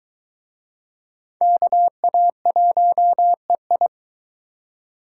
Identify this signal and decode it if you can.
Morse code, heard as a keyed tone, about 22 wpm, 700 Hz. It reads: KA1EI